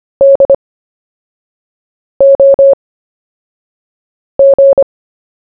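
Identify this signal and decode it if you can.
Morse code, heard as a keyed tone, about 25 wpm, spaced as for 5 wpm, 550 Hz. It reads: DOG
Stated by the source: CW story-telling practice, Short Story 1 sent at 5 wpm, by VK4JE, Queensland, Australia.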